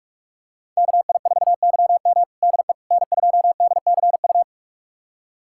Morse code, heard as a keyed tone, about 43 wpm, 700 Hz. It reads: KI4YM DE N2DCU
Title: USA Callsign DE Callsign - 5 Characters - 45wpm